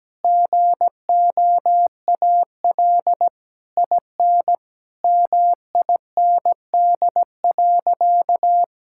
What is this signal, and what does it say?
Morse code, heard as a keyed tone, about 17 wpm, 700 Hz. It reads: GOAL IN MIND.
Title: Sentences from Top 1000 Words - Review Part 5 - 17wpm